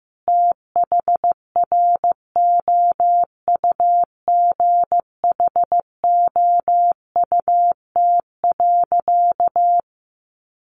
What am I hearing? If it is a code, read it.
Morse code, heard as a keyed tone, about 15 wpm, 700 Hz. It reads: THROUGHOUT.
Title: Sentences from Top 800 Words 15wpm